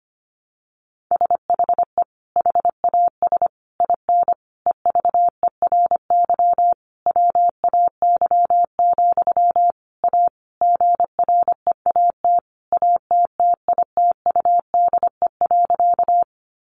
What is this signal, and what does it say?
Morse code, heard as a keyed tone, about 25 wpm, 700 Hz. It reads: SHE HAS IN EVERY WAY, A GREAT ATTITUDE.
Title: Sentences from Top 1300 Words 25wpm